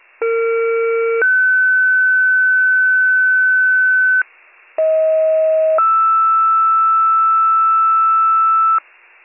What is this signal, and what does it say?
Two-tone page, heard as a one-second tone, then a three-second tone, about 460 then 1500 Hz, about 640 then 1320 Hz, with noise. It is heard from 0.2 to 8.8 s.